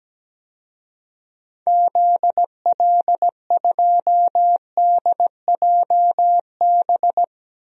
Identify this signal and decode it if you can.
Morse code, heard as a keyed tone, about 17 wpm, 700 Hz. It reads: ZL2DJB
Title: DXCC-100 Call Signs - 6 Characters - 17wpm